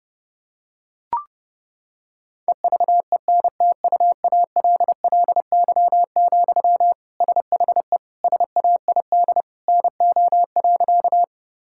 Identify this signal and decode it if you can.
Morse code, heard as a keyed tone, about 32 wpm, 700 Hz. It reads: EVENTUALLY, SHE SAID NO.
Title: Sentences from Top 1000 Words - Review Part 6 - 30wpm